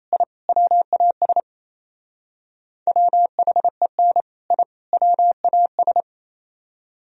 Morse code, heard as a keyed tone, about 34 wpm, 700 Hz. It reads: I WAS WHEN I WAS
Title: Speed-Racing - Sets of 3 Words 28wpm